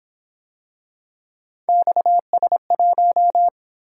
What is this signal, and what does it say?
Morse code, heard as a keyed tone, about 26 wpm, 700 Hz. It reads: XS1
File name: Speed-Racing - Three Letter-Number Combinations - 22wpm